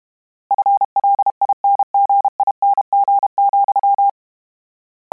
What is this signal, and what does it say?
Morse code, heard as a keyed tone, about 32 wpm, 800 Hz. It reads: FLINGING,
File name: Combinations / Combinations-mp3s / Combinations@30